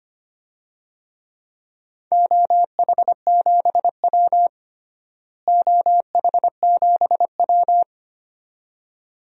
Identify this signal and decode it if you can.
Morse code, heard as a keyed tone, about 25 wpm, 700 Hz. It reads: OH7W OH7W
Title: DXCC-100 Call Signs - 4 Characters - Repeated 2x - 25wpm